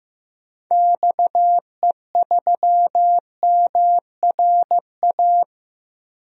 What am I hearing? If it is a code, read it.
Morse code, heard as a keyed tone, about 15 wpm, 700 Hz. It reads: XE3MRA
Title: DXCC-100 Call Signs - 6 Characters - 15wpm